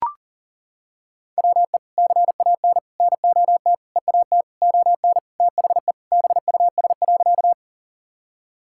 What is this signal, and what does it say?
Morse code, heard as a keyed tone, about 40 wpm, 700 Hz. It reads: WE CAN NOT EAT ON THE BUS.
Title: Sentences from Top 1000 Words - Review Part 6 - 40wpm